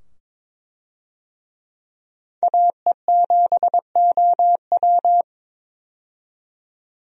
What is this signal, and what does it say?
Morse code, heard as a keyed tone, about 22 wpm, 700 Hz. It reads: AE7OW
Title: USA Call Signs - 5 Characters - 22wpm